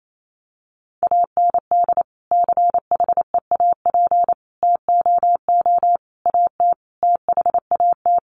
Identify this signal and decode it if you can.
Morse code, heard as a keyed tone, about 28 wpm, 700 Hz. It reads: AND CHEAP TOO AT THAT
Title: Sentences from Top 1200 Words 28wpm